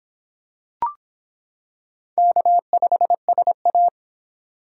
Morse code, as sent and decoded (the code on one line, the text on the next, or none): -.- ..... ... .-
K5SA